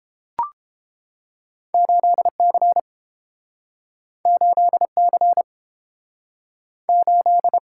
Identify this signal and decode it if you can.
Morse code, heard as a keyed tone, about 30 wpm, 700 Hz. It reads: 8C 8C 8